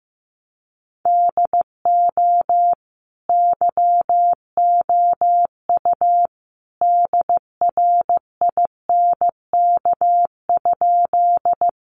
Morse code, heard as a keyed tone, about 15 wpm, 700 Hz. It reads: DO YOU DRINK?
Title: Sentences from Top 1500 Words 15wpm